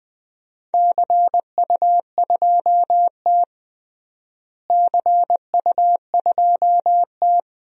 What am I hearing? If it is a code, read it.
Morse code, heard as a keyed tone, about 20 wpm, 700 Hz. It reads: CU2T CU2T
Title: DXCC-100 Call Signs - 4 Characters - Repeated 3x - 20wpm